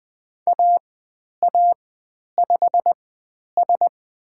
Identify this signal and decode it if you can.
Morse code, heard as a keyed tone, about 20 wpm, spaced as for 10 wpm, 700 Hz. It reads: AA5S
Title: 109 Lesson 18g - Letter m - Call Signs 20wpm